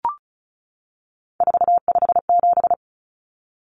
Morse code, heard as a keyed tone, about 34 wpm, 700 Hz. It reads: 457
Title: Numbers - 3-Digit 35wpm